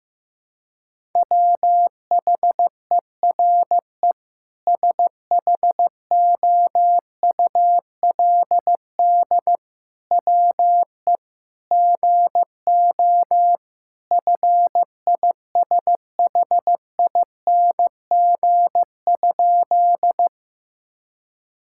Morse code, heard as a keyed tone, about 15 wpm, 700 Hz. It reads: WHERE SHOULD WE GO FISHING?
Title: Sentences from Top 1000 Words - Review Part 5 - 15wpm